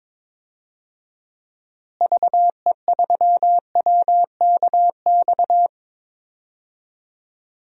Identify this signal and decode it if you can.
Morse code, heard as a keyed tone, about 22 wpm, 700 Hz. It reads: VE3WKX
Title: DXCC-100 Call Signs - 6 Characters - Repeated 3x - 22wpm